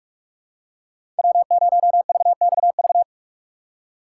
Morse code, heard as a keyed tone, about 45 wpm, 700 Hz. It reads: W0VXV
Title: USA Call Signs - 5 Characters - 3x Character Spacing - 45wpm